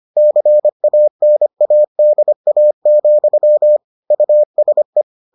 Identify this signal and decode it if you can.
Morse code, heard as a keyed tone, about 25 wpm, 600 Hz. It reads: CANADA, USE